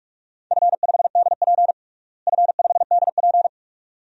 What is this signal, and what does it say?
Morse code, heard as a keyed tone, about 47 wpm, 700 Hz. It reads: F5DP F5DP
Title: DXCC-100 Call Signs - 4 Characters - Repeated 2x - 45wpm